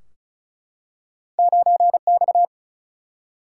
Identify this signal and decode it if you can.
Morse code, heard as a keyed tone, about 36 wpm, 700 Hz. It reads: 9X
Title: Single-Letter Single-Number 35wpm